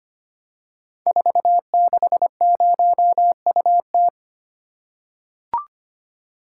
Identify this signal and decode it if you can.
Morse code, heard as a keyed tone, about 25 wpm, 700 Hz. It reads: 460UT
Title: Code Groups - 5 Letter:Numbers - 25wpm